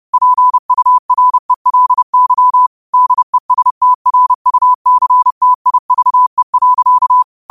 Morse code, heard as a keyed tone, about 30 wpm, 1000 Hz. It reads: PURELY DESTRUCTIVE.